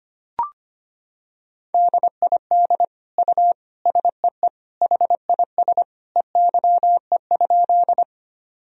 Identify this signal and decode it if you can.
Morse code, heard as a keyed tone, about 25 wpm, 700 Hz. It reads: DID U SEE HIS EYE?
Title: Sentences from Top 300 Words and Abbreviations 25wpm